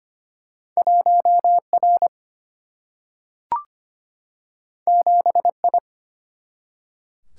Morse code, heard as a keyed tone, about 25 wpm, 700 Hz. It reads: 1R 7I
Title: Single-Letter Single-Number 25wpm